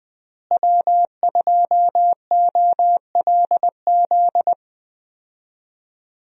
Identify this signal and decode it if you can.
Morse code, heard as a keyed tone, about 20 wpm, 700 Hz. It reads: W2OLZ
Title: USA Call Signs - 5 Characters - Repeated 3x - 20wpm